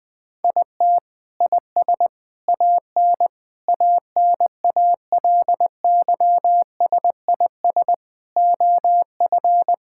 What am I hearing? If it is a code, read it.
Morse code, heard as a keyed tone, about 20 wpm, 700 Hz. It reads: IT IS AN ANALYSIS OF
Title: Sentences from Top 1000 Words - Review Part 5 - 20wpm